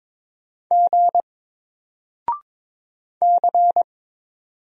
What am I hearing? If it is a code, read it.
Morse code, heard as a keyed tone, about 22 wpm, 700 Hz. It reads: G C
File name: Single-Letter-Number-22wpm